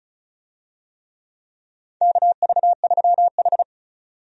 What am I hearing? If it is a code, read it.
Morse code, heard as a keyed tone, about 34 wpm, 700 Hz. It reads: KV3H
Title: USA Call Signs - 4 Characters - 35wpm